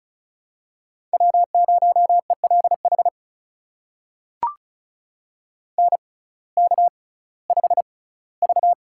Morse code, heard as a keyed tone, about 36 wpm, spaced as for 19 wpm, 700 Hz. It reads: W0ELH NK5V
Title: USA Call Signs - 6x Character Spacing - 35wpm